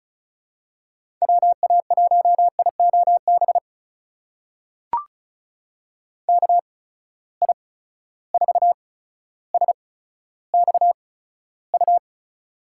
Morse code, heard as a keyed tone, about 34 wpm, spaced as for 15 wpm, 700 Hz. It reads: WA1IOB KI4SXU
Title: USA Call Signs - 6 Characters - 8x Character Spacing - 35wpm